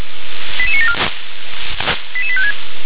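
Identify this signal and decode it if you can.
Five-tone selective call (ZVEI), heard as quick runs of five tones, tones of about 70 ms. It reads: 88974 88856
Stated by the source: ZVEI five-tone selective call received via SDR.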